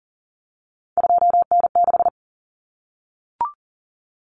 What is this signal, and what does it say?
Morse code, heard as a keyed tone, about 40 wpm, 700 Hz. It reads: 2N6
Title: Common Cut Numbers - 3-Digit 40wpm